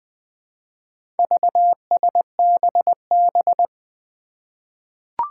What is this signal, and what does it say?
Morse code, heard as a keyed tone, about 20 wpm, 700 Hz. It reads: VSBB
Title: Four Challenging Letter-Number Combinations 20wpm